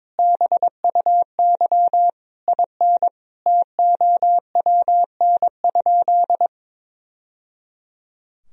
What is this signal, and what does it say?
Morse code, heard as a keyed tone, about 22 wpm, 700 Hz. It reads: BUY IN TOWN?
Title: Sentences from Top 1000 Words - Review Part 4 - 22wpm